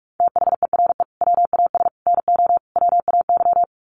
Morse code, heard as a keyed tone, about 44 wpm, 700 Hz. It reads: THERE WAS NO WAY
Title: Sets of 5 Words Encore 45wpm